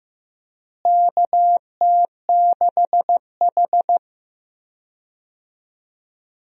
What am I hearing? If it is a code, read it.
Morse code, heard as a keyed tone, about 15 wpm, 700 Hz. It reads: KT6H
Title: USA Call Signs - 4 Characters - Repeated 3x - 15wpm